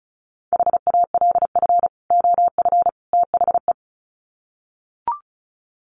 Morse code, heard as a keyed tone, about 34 wpm, 700 Hz. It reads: HALF OF THE
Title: Sets of 3 Words 35wpm